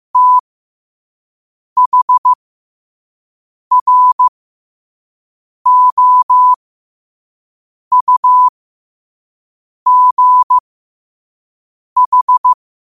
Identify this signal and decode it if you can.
Morse code, heard as a keyed tone, about 15 wpm, spaced as for 5 wpm, 1000 Hz. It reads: THROUGH
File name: qotdfiles / 5wpm